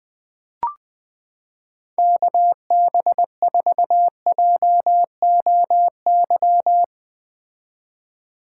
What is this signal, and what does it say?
Morse code, heard as a keyed tone, about 20 wpm, 700 Hz. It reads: KB4JOY
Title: 2023 POTA Activator Call Signs - 20wpm